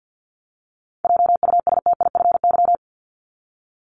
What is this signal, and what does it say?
Morse code, heard as a keyed tone, about 48 wpm, 700 Hz. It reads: JUSTIFY
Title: Top 1751-2000 Words 50wpm